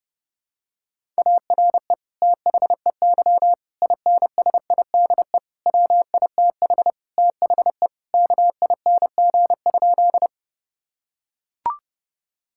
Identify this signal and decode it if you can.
Morse code, heard as a keyed tone, about 32 wpm, 700 Hz. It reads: ARE THEY INSIDE WITH THE KING?
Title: Sentences from Top 1000 Words - Review Part 6 - 30wpm